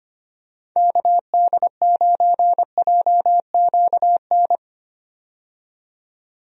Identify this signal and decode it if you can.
Morse code, heard as a keyed tone, about 25 wpm, 700 Hz. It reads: KD9JQN
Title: USA Call Signs - 6 Characters - 25wpm